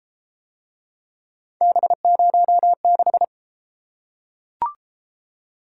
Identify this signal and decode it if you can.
Morse code, heard as a keyed tone, about 28 wpm, 700 Hz. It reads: B06